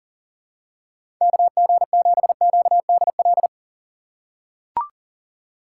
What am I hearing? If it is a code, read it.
Morse code, heard as a keyed tone, about 40 wpm, 700 Hz. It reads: KG7QDL
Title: USA Call Signs - 6 Characters - 3x Character Spacing - 40wpm